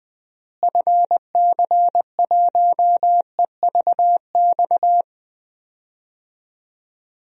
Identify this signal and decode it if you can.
Morse code, heard as a keyed tone, about 20 wpm, 700 Hz. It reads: FC1EVX
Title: DXCC-100 Call Signs - 6 Characters - Repeated 3x - 20wpm